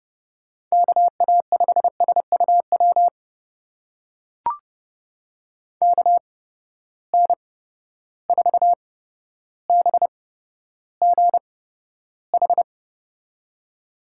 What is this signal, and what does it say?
Morse code, heard as a keyed tone, about 31 wpm, spaced as for 13 wpm, 700 Hz. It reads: KA5SUW KN4BGH
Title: USA Call Signs - 6 Characters - 8x Character Spacing - 30wpm